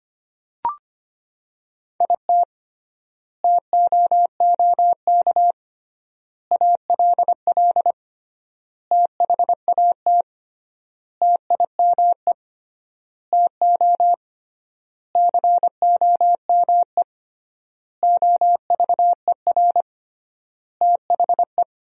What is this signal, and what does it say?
Morse code, heard as a keyed tone, about 25 wpm, 700 Hz. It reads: IT TOOK ALL THAT TIME TO COME OVER THE
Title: 3x Word-Spacing - Sentences from Top 300 Words - 25wpm